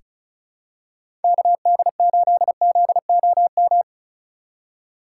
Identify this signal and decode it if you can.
Morse code, heard as a keyed tone, about 34 wpm, 700 Hz. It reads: KD8ZOM
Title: USA Call Signs - 6 Characters - 4x Character Spacing - 35wpm